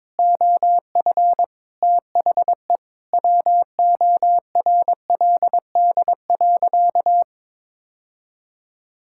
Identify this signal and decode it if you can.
Morse code, heard as a keyed tone, about 22 wpm, 700 Hz. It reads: OF THE WORLD.
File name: practice / 22wpm / Sentences with 3-Word Phrases - 22wpm